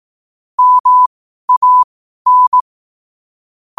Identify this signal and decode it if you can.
Morse code, heard as a keyed tone, about 18 wpm, spaced as for 12 wpm, 1000 Hz. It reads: MAN
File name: qotdfiles / 12wpm